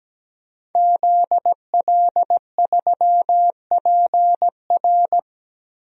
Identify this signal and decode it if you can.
Morse code, heard as a keyed tone, about 17 wpm, 700 Hz. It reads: ZL3PR
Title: DXCC-100 Call Signs - 5 Characters - 17wpm